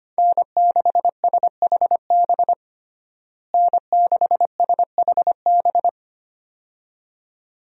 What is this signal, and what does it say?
Morse code, heard as a keyed tone, about 25 wpm, 700 Hz. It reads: N6SHB N6SHB
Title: USA Call Signs - 5 Characters - Repeated 2x - 25wpm